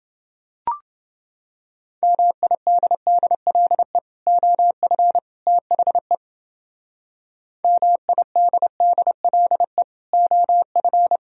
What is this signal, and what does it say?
Morse code, heard as a keyed tone, about 29 wpm, 700 Hz. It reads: MIDDLE OF THE MIDDLE OF